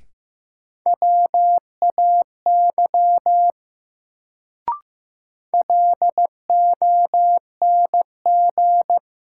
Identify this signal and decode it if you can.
Morse code, heard as a keyed tone, about 15 wpm, 700 Hz. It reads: WAY LONG
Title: Sentences from Top 100 Words - New Word Drill - 15wpm